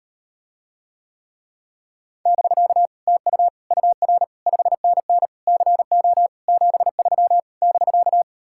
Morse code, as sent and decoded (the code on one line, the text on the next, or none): -...-.- / - ..- / ..- .-. / ..... -. -. / -.-. --- / --... ...-- / -...-.-
<BK> TU UR 5NN CO 73 <BK>